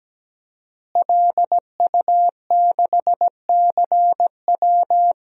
Morse code, heard as a keyed tone, about 17 wpm, 700 Hz. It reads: LU6CW
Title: DXCC-100 Call Signs - 5 Characters - Repeated 2x - 17wpm